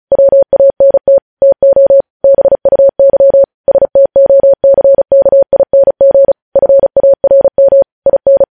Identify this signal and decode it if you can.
Morse code, heard as a keyed tone, about 36 wpm, 550 Hz. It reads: WANT TO BUY STOCKING FARM IN